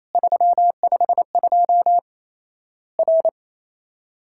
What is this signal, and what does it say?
Morse code, heard as a keyed tone, about 28 wpm, 700 Hz. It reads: 352 R
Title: Worked All Europe DX Contest - QTC Exchange - 28wpm